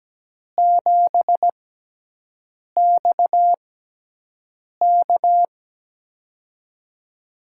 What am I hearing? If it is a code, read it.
Morse code, heard as a keyed tone, about 17 wpm, spaced as for 6 wpm, 700 Hz. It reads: 7XK
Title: USA Call Signs - 4 Characters - 6x Character Spacing - 17wpm